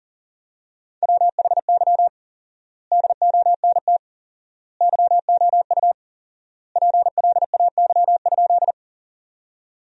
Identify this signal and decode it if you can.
Morse code, heard as a keyed tone, about 40 wpm, 700 Hz. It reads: WHY DONT YOU PLAY?
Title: 4x Word-Spacing - Sentences from Top 300 Words - 40wpm